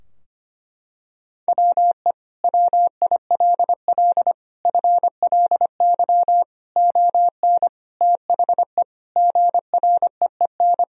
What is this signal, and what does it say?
Morse code, heard as a keyed tone, about 25 wpm, 700 Hz. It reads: WE WILL FLY ON THE GREEN
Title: Sentences from Top 1000 Words - Review Part 3 - 25wpm